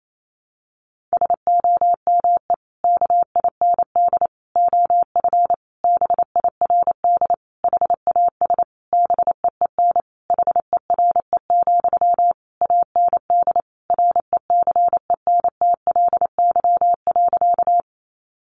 Morse code, heard as a keyed tone, about 28 wpm, 700 Hz. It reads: SOME KIND OF BIRD HAS BEEN HERE, AND RECENTLY.
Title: Sentences from Top 1200 Words 28wpm